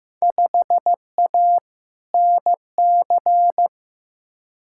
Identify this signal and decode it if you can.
Morse code, heard as a keyed tone, about 15 wpm, 700 Hz. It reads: 5A NC
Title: Field Day Exchanges - 15wpm